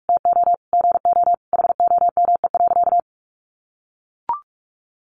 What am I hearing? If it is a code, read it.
Morse code, heard as a keyed tone, about 44 wpm, 700 Hz. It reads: TO GO HOME.
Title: Sentences from Top 1400 Words 45wpm